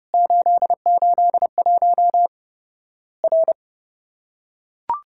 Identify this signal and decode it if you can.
Morse code, heard as a keyed tone, about 30 wpm, 700 Hz. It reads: 881 R
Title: Worked All Europe DX Contest - QTC Exchange - 30wpm